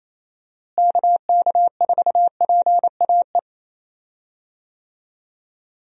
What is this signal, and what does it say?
Morse code, heard as a keyed tone, about 28 wpm, 700 Hz. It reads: KK4PAE
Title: USA Call Signs - 6 Characters - Repeated 3x - 28wpm